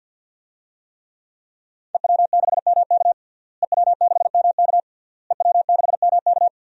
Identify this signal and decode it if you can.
Morse code, heard as a keyed tone, about 52 wpm, 700 Hz. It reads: EW6MK EW6MK EW6MK
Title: DXCC-100 Call Signs - 5 Characters - Repeated 3x - 50wpm